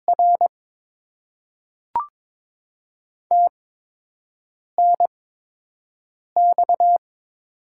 Morse code, heard as a keyed tone, about 22 wpm, 700 Hz. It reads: R T N X
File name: Top 20 QSO Elements - 8x Character Spacing - 22wpm